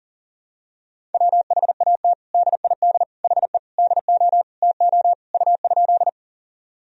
Morse code, heard as a keyed tone, about 40 wpm, 700 Hz. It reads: WHAT DID HE DO TO U?
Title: Sentences from Top 100 Words and Abbreviations 40wpm